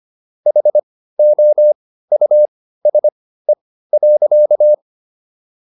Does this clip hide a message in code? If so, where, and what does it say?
Yes: Morse code from 0.5 to 4.7 s, heard as a keyed tone, about 25 wpm, spaced as for 15 wpm, 600 Hz. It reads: HOUSE.